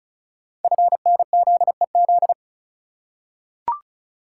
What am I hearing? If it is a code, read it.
Morse code, heard as a keyed tone, about 36 wpm, 700 Hz. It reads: FNZEZ
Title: Code Groups - 5 Letters - 35wpm